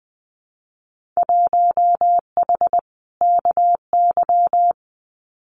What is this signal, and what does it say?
Morse code, heard as a keyed tone, about 20 wpm, 700 Hz. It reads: 1H KY
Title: Winter Field Day Exchanges 20wpm